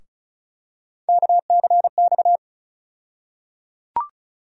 Morse code, heard as a keyed tone, about 36 wpm, 700 Hz. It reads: KCX
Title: USA Call Sign Postfixes - 3 Letters 35wpm